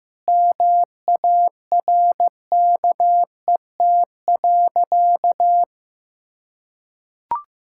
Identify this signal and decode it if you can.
Morse code, heard as a keyed tone, about 16 wpm, 700 Hz. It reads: MARKET.
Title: Sentences from Top 1000 Words - Review Part 6 - 15wpm